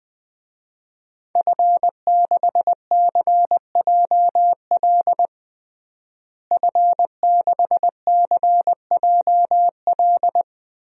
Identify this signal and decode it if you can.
Morse code, heard as a keyed tone, about 20 wpm, 700 Hz. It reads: F6CJL F6CJL
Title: DXCC-100 Call Signs - 5 Characters - Repeated 3x - 20wpm